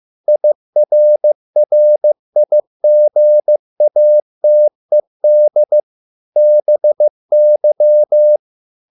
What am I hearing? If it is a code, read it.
Morse code, heard as a keyed tone, about 15 wpm, 600 Hz. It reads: IRRIGATED BY